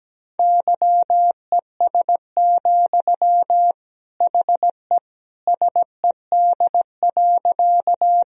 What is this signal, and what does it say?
Morse code, heard as a keyed tone, about 17 wpm, 700 Hz. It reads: YES, HE SED.
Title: Sentences from Top 500 Words and Abbreviations 17wpm